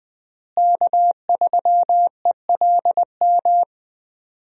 Morse code, heard as a keyed tone, about 20 wpm, 700 Hz. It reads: K3ELM
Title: USA Call Signs - 5 Characters - Repeated 2x - 20wpm